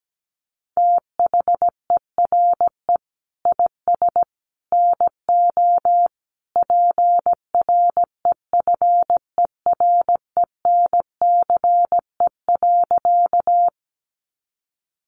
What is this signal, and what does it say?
Morse code, heard as a keyed tone, about 17 wpm, 700 Hz. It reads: THERE IS NO PREFERENCE.
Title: Sentences from Top 1500 Words 17wpm